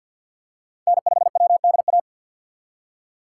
Morse code, heard as a keyed tone, about 53 wpm, 700 Hz. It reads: N5WDA
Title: USA Call Signs - 6x Character Spacing - 50wpm